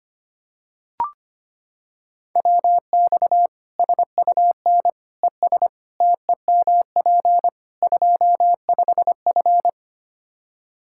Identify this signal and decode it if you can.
Morse code, heard as a keyed tone, about 25 wpm, 700 Hz. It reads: WX SUN ES TEMP 25F
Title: Practice QSO - Exchange 2 - Info - 25wpm